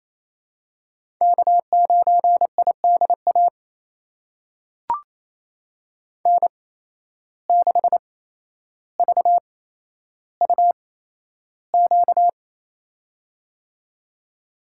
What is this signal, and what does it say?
Morse code, heard as a keyed tone, about 28 wpm, spaced as for 12 wpm, 700 Hz. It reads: K9IDA N6VUQ
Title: USA Call Signs - 5 Characters - 8x Character Spacing - 28wpm